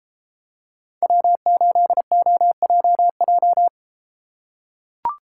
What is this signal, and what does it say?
Morse code, heard as a keyed tone, about 32 wpm, 700 Hz. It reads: W8OJJ